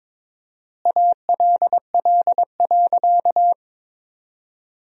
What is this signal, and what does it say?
Morse code, heard as a keyed tone, about 22 wpm, 700 Hz. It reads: ALL.
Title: 4x Word-Spacing - Sentences from Top 200 Words - 22wpm